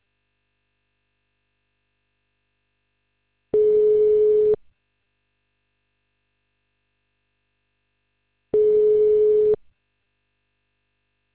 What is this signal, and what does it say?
Ringback tone (Europe, 425 Hz).